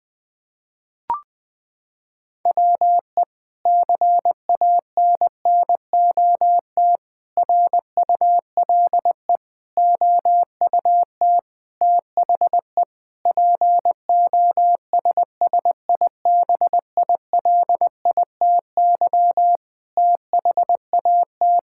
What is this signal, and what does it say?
Morse code, heard as a keyed tone, about 21 wpm, 700 Hz. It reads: WE CANNOT RULE OUT THE POSSIBILITY THAT
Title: Sentences with 8-Word Phrases - 20wpm